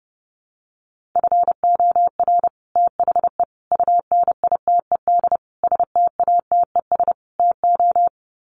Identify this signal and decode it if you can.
Morse code, heard as a keyed tone, about 30 wpm, 700 Hz. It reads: FOR THE UNITED STATES TO
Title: Sets of 5 Words Encore 30wpm